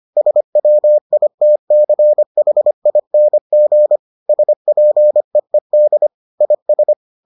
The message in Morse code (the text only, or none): SWITCHING SPEED IS